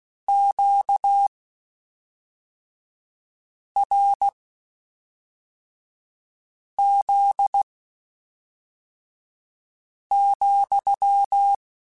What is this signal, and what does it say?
Morse code, heard as a keyed tone, about 16 wpm, spaced as for 3 wpm, 800 Hz. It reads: QRZ,
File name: Q Calls / Q Calls-mp3s / Q CALLS SPSL @5